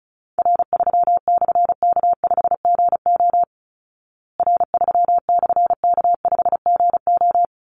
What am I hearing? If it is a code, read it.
Morse code, heard as a keyed tone, about 34 wpm, 700 Hz. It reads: R3/K5GO R3/K5GO
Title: Call Signs Repeated 3x - 35wpm